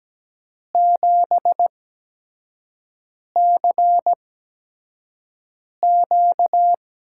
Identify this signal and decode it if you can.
Morse code, heard as a keyed tone, about 17 wpm, spaced as for 5 wpm, 700 Hz. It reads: 7CQ
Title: USA Call Signs - 6 Characters - 8x Character Spacing - 17wpm